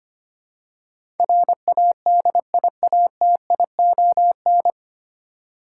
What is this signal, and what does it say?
Morse code, heard as a keyed tone, about 25 wpm, 700 Hz. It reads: RADIATION
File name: Nine Letter Words - 25wpm